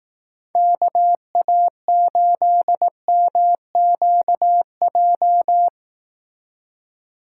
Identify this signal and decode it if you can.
Morse code, heard as a keyed tone, about 18 wpm, 700 Hz. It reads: KA8MQJ
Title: Speed Racing - USA 2x3 Call Signs 15wpm